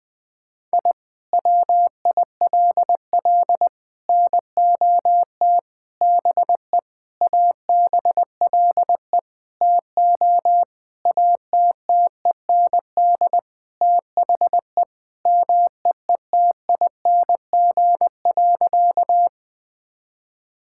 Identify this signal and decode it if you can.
Morse code, heard as a keyed tone, about 20 wpm, 700 Hz. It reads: I WILL NOT BE ABLE TO ATTEND THE MEETING.
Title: Sentences with 5-Word Phrases - Spoken Beginning - 20wpm